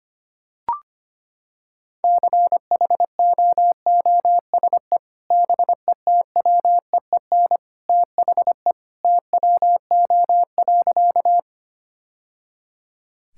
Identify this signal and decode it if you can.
Morse code, heard as a keyed tone, about 25 wpm, 700 Hz. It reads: CHOOSE BETWEEN THE TWO.
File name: practice / 25wpm / Sentences with 2-Word Phrases - 25wpm